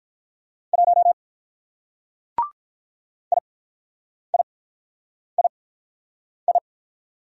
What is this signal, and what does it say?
Morse code, heard as a keyed tone, about 43 wpm, spaced as for 9 wpm, 700 Hz. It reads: 1 IIII
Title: Speed-Racing Single Letter-Number 35wpm